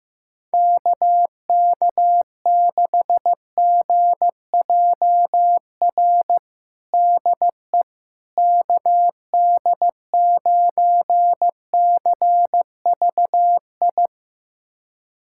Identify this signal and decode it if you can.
Morse code, heard as a keyed tone, about 16 wpm, 700 Hz. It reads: KK6GJR DE KD9CVI
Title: USA Callsign DE Callsign - 6 Characters - 15wpm